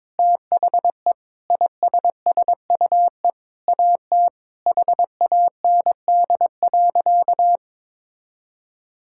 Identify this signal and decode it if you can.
Morse code, heard as a keyed tone, about 22 wpm, 700 Hz. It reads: THE ISSUE AT HAND.
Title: Sentences from Top 1000 Words - Review Part 6 - 22wpm